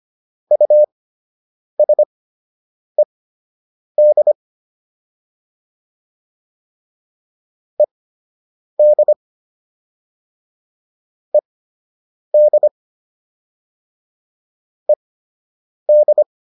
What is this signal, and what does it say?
Morse code, heard as a keyed tone, about 25 wpm, spaced as for 8 wpm, 600 Hz. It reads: USED ED ED ED